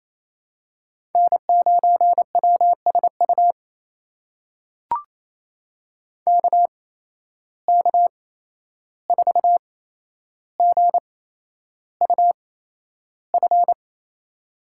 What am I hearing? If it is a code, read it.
Morse code, heard as a keyed tone, about 28 wpm, spaced as for 11 wpm, 700 Hz. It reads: N9WSU KK4GUF
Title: USA Call Signs - 8x Character Spacing - 28wpm